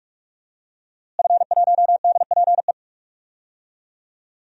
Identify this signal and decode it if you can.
Morse code, heard as a keyed tone, about 41 wpm, 700 Hz. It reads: F1DPE
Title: DXCC-100 Call Signs - 5 Characters - Repeated 3x - 45wpm